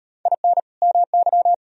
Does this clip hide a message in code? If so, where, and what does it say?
Yes: Morse code from 0.3 to 1.5 s, heard as a keyed tone, about 36 wpm, 700 Hz. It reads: IN MY